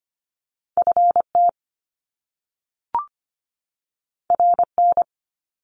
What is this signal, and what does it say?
Morse code, heard as a keyed tone, about 25 wpm, 700 Hz. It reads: FT RN
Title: Top Two Letter Combinations in Words 25wpm